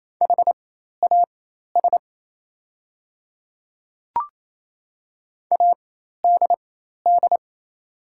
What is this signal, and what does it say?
Morse code, heard as a keyed tone, about 27 wpm, spaced as for 13 wpm, 690 Hz. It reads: HAS ADD
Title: 4x Character Spacing - Top 200 Words - 28wpm